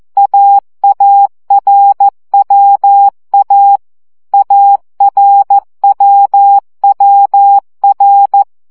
Morse code, heard as a keyed tone, about 14 wpm, 800 Hz. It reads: AARWA ARWWR
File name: CW gyakorlas a r w 12 baud